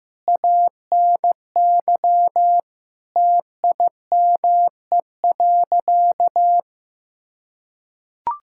Morse code, heard as a keyed tone, about 16 wpm, 700 Hz. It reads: ANY TIME.
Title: Sentences from Top 1000 Words - Review Part 6 - 15wpm